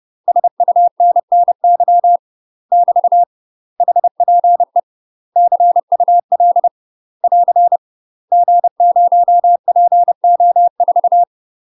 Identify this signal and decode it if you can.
Morse code, heard as a keyed tone, about 30 wpm, 700 Hz. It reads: SUNNY <BT> HPE CUL <AR> G0PO4
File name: UK MASTER QSOs TRAINING / 30WPM / UK MASTER QSOs 0047